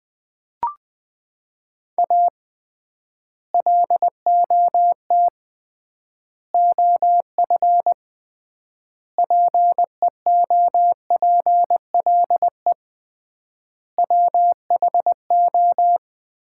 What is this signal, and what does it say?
Morse code, heard as a keyed tone, about 21 wpm, 700 Hz. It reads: A LOT OF PEOPLE WHO
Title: 3x Word-Spacing - Sets of 5 Words - 20wpm